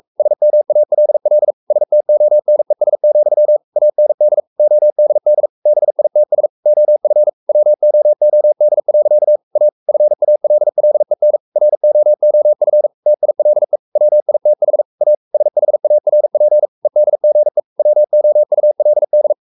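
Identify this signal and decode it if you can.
Morse code, heard as a keyed tone, about 44 wpm, 600 Hz. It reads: SMALL STONES, AND ODD BITS OF WOOD. A FALLEN ROOF TILE WITH A SHARP EDGE WOULD
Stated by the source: Morse code at 43 wpm sending Ken Follett's novel Never in English, from CW med Gnister.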